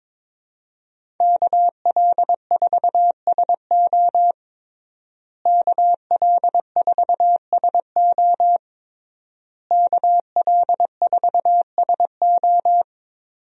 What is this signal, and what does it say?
Morse code, heard as a keyed tone, about 22 wpm, 700 Hz. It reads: KL4SO KL4SO KL4SO
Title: USA Call Signs - 5 Characters - Repeated 3x - 22wpm